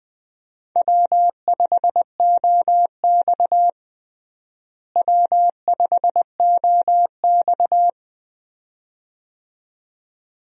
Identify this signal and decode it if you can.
Morse code, heard as a keyed tone, about 21 wpm, 700 Hz. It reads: W5OX W5OX
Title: USA Call Signs - 4 Characters - Repeated 2x - 20wpm